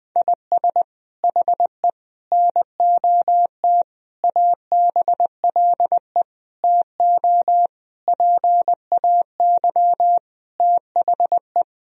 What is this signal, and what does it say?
Morse code, heard as a keyed tone, about 21 wpm, 700 Hz. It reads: IS HE NOT ABLE TO PAY THE
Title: Sentences from Top 1000 Words - Review Part 2 - 20wpm